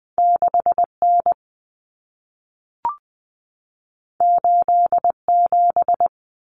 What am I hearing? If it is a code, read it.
Morse code, heard as a keyed tone, about 20 wpm, 700 Hz. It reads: 6N 87
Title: Common Cut Numbers - 2-Digit 20wpm